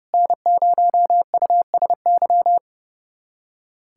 Morse code, heard as a keyed tone, about 32 wpm, 700 Hz. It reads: N0USY
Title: USA Callsign DE Callsign - 5 Characters - 30wpm